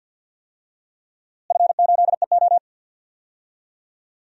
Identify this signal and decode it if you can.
Morse code, heard as a keyed tone, about 48 wpm, 700 Hz. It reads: F8EO
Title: DXCC-100 Call Signs - 4 Characters - Repeated 3x - 50wpm